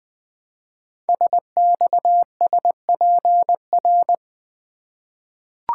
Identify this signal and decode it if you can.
Morse code, heard as a keyed tone, about 20 wpm, 700 Hz. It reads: SXSPR
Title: Code Groups - 5 Letters - 20wpm